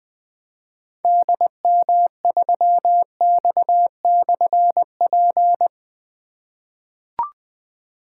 Morse code, heard as a keyed tone, about 20 wpm, 700 Hz. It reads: DM3X/P
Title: Difficult Call Signs 20wpm